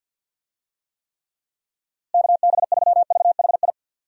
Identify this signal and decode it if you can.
Morse code, heard as a keyed tone, about 48 wpm, 700 Hz. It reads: KB3VHI
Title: USA Call Signs - 6 Characters - Repeated 2x - 50wpm